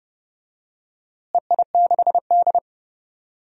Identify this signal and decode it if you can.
Morse code, heard as a keyed tone, about 32 wpm, 700 Hz. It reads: EI6D
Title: DXCC-100 Call Signs - 4 Characters - Repeated 2x - 30wpm